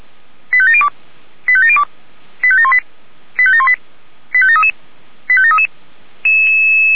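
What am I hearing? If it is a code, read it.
Five-tone selective call (ZVEI), heard as quick runs of five tones, tones of about 70 ms. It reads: 85702 85702 85718 85718 85733 85733
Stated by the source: ZVEI five-tone selective call 85796 received on SDR.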